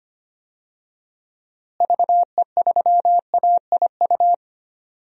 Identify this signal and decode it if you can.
Morse code, heard as a keyed tone, about 25 wpm, 700 Hz. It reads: VE3AIU
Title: DXCC-100 Call Signs - 6 Characters - Repeated 2x - 25wpm